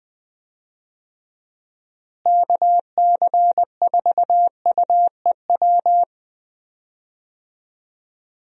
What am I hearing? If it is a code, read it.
Morse code, heard as a keyed tone, about 20 wpm, 700 Hz. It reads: KC4UEW
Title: USA Call Signs - 6 Characters - Repeated 2x - 20wpm